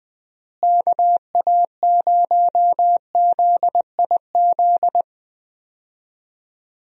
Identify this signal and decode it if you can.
Morse code, heard as a keyed tone, about 20 wpm, 700 Hz. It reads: KA0ZIZ
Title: USA Call Signs - 6 Characters - Repeated 3x - 20wpm